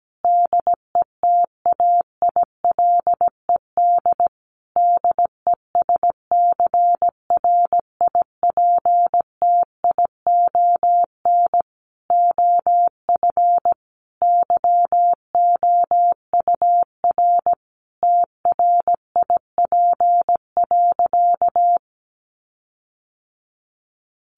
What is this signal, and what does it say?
Morse code, heard as a keyed tone, about 17 wpm, 700 Hz. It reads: DETAILED DESCRIPTION OF YOUR TRIP.
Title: Sentences from Top 1500 Words 17wpm